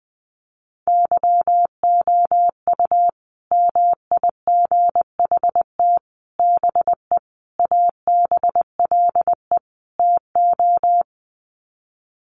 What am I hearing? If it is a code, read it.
Morse code, heard as a keyed tone, about 20 wpm, 700 Hz. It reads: YOU MIGHT BE ABLE TO
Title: Sets of 5 Words Encore2 20wpm